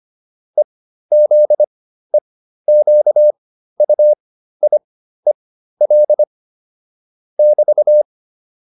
Morse code, heard as a keyed tone, about 25 wpm, spaced as for 13 wpm, 600 Hz. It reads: EZEQUIEL <BT>